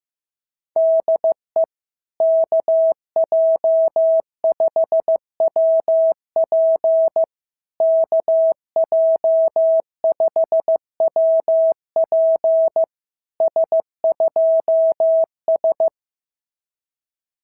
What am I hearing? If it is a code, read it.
Morse code, heard as a keyed tone, about 15 wpm, 650 Hz. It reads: DE KJ5WP KJ5WP S2S
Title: SOTA QSO Practice 15wpm